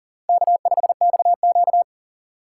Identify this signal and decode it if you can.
Morse code, heard as a keyed tone, about 40 wpm, 700 Hz. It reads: K5XQ
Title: K1USN SST QSO Practice - 40wpm